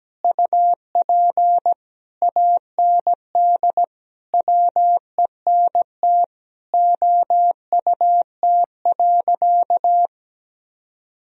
Morse code, heard as a keyed tone, about 17 wpm, 700 Hz. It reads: UP AND WENT OUT.
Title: Sentences from Top 1000 Words - Review Part 1 - 17wpm